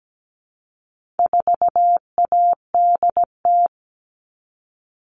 Morse code, heard as a keyed tone, about 17 wpm, 700 Hz. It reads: 4ADT
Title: All Cut Numbers - 4-Digit 17wpm